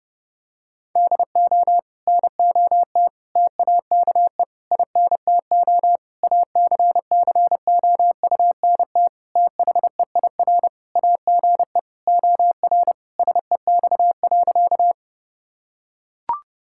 Morse code, heard as a keyed tone, about 30 wpm, 700 Hz. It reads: DO NOT TAKE INTO ACCOUNT THEIR AGE OR SEX.
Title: Sentences from Top 1600 Words 30wpm